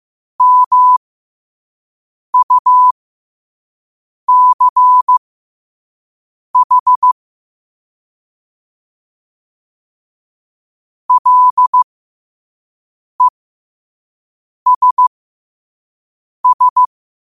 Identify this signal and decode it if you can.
Morse code, heard as a keyed tone, about 15 wpm, spaced as for 5 wpm, 1000 Hz. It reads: MUCH LESS